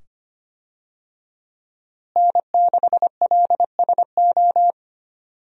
Morse code, heard as a keyed tone, about 25 wpm, 700 Hz. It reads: N6LSO
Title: USA Call Signs - 5 Characters - Repeated 2x - 25wpm